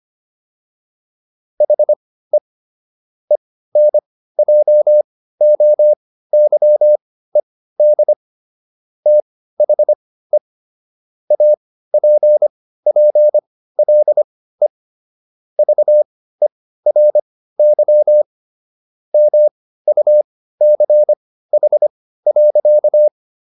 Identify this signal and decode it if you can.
Morse code, heard as a keyed tone, about 25 wpm, spaced as for 15 wpm, 600 Hz. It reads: HE ENJOYED THE APPLE VERY MUCH.